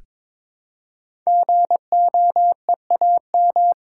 Morse code, heard as a keyed tone, about 22 wpm, 700 Hz. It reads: GOEAM
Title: Code Groups - 5 Letters - 22wpm